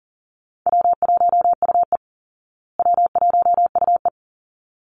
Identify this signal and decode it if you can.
Morse code, heard as a keyed tone, about 40 wpm, 700 Hz. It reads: W1UE W1UE
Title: Easy Call Signs Repeated 2x - 40wpm